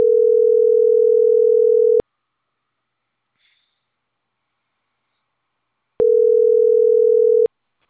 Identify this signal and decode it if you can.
Ringback tone (North America), 440 + 480 Hz.